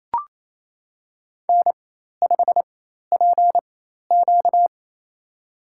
Morse code, heard as a keyed tone, about 28 wpm, spaced as for 13 wpm, 700 Hz. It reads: N5PQ
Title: USA Call Signs - 4x Character Spacing - 28wpm